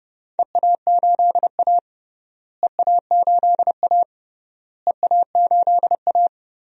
Morse code, heard as a keyed tone, about 30 wpm, 700 Hz. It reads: EA8A EA8A EA8A
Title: DXCC-100 Call Signs - 4 Characters - Repeated 3x - 30wpm